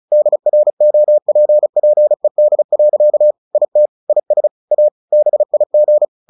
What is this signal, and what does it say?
Morse code, heard as a keyed tone, about 34 wpm, 600 Hz. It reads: DROPPED. IT IS A BIG
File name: the old man and the sea / the old man and the sea 1 35wpm